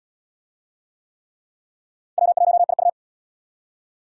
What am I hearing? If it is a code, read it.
Morse code, heard as a keyed tone, about 66 wpm, 700 Hz. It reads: W1IU